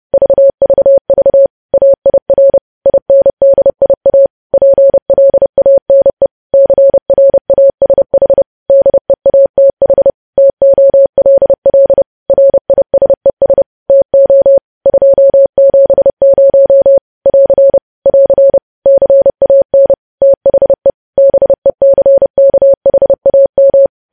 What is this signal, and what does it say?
Morse code, heard as a keyed tone, about 30 wpm, 550 Hz. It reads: VVV AIR INDIA PLANE CRASH DEATH TOLL RISES TO 270 <AR> <AR> CAN THE BECKHAM
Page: news headlines in morse code at 30 wpm